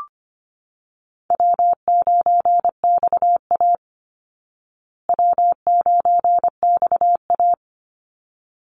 Morse code, heard as a keyed tone, about 25 wpm, 700 Hz. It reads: W9XA W9XA
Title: Call Signs Repeated 3x - 25wpm